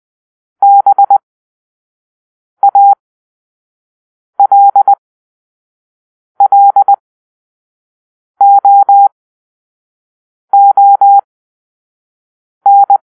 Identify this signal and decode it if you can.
Morse code, heard as a keyed tone, about 20 wpm, spaced as for 5 wpm, 800 Hz. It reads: BALLOON
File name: arrl / Grp-4 Drill-1